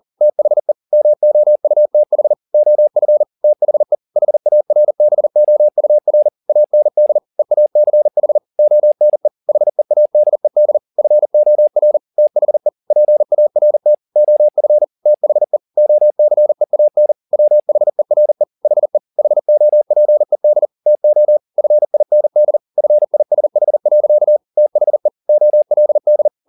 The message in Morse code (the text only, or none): THE MOUTH OF THE HARBOUR AND EACH ONE HEADED FOR THE PART OF THE OCEAN WHERE HE HOPED TO FIND FISH. THE OLD